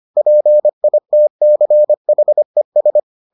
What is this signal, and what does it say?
Morse code, heard as a keyed tone, about 25 wpm, 600 Hz. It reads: PITCHES